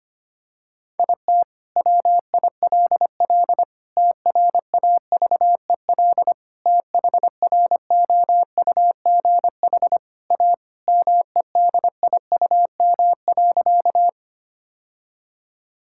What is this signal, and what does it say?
Morse code, heard as a keyed tone, about 25 wpm, 700 Hz. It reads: IT WILL TRAVEL THROUGH A MEDIUM.
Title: Sentences from Top 1000 Words - Review Part 6 - 25wpm